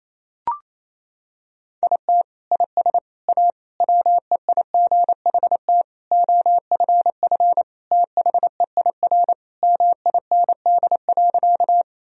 Morse code, heard as a keyed tone, about 28 wpm, 700 Hz. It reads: IT IS A WEIGHT OFF THEIR MIND.
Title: Sentences from Top 1000 Words - Review Part 3 - 28wpm